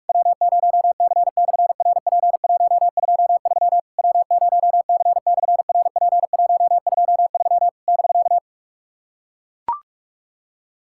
Morse code, heard as a keyed tone, about 44 wpm, 700 Hz. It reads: W0C/RP123 W0C/RP123 <BK>
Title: SOTA QSO Practice 45wpm